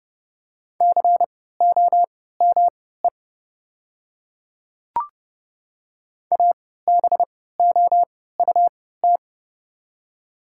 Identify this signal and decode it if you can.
Morse code, heard as a keyed tone, about 30 wpm, spaced as for 17 wpm, 700 Hz. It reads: COME ABOUT
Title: 3x Character Spacing - Top 100 Words - 30wpm